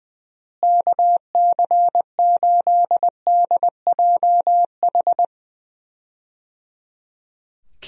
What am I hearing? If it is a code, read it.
Morse code, heard as a keyed tone, about 20 wpm, 700 Hz. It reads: KC8DJH
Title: USA Call Signs - 6 Characters - Repeated 3x - 20wpm